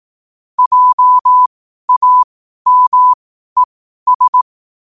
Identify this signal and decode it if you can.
Morse code, heard as a keyed tone, about 18 wpm, spaced as for 12 wpm, 1000 Hz. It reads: JAMES
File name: qotdfiles / 12wpm